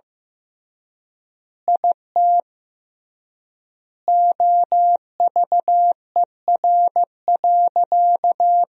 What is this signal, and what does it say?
Morse code, heard as a keyed tone, about 15 wpm, 700 Hz. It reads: IT OVER.
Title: 3x Word-Spacing - Sentences from Top 200 Words - 15wpm